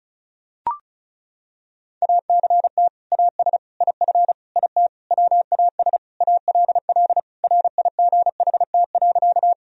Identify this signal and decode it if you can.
Morse code, heard as a keyed tone, about 34 wpm, 700 Hz. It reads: ACT AS IF IT WAS ALL RIGHT.
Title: Sentences from Top 1000 Words - Review Part 1 - 35wpm